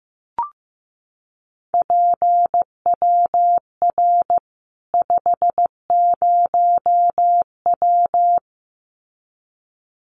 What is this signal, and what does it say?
Morse code, heard as a keyed tone, about 15 wpm, 700 Hz. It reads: PWR 50W
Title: Common QSO Phrases 15wpm